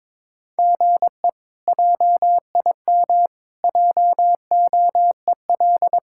Morse code, heard as a keyed tone, about 21 wpm, 700 Hz. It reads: GE JIM JOEL
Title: K1USN SST QSO Practice - 22wpm